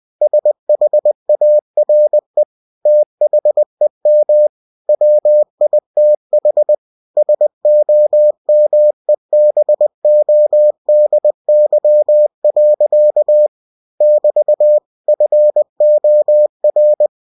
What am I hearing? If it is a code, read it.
Morse code, heard as a keyed tone, about 20 wpm, 600 Hz. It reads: SHARE THEM WITH SOMEBODY. <BT> FOR
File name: cw-books / winnie-the-pooh / winnie-20wpm-page-0091